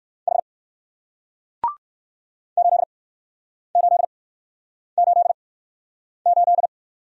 Morse code, heard as a keyed tone, about 60 wpm, spaced as for 9 wpm, 700 Hz. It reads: H 8888